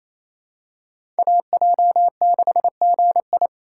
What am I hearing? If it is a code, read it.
Morse code, heard as a keyed tone, about 28 wpm, 700 Hz. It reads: AJ6GI
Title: USA Call Signs - 5 Characters - 4x Character Spacing - 28wpm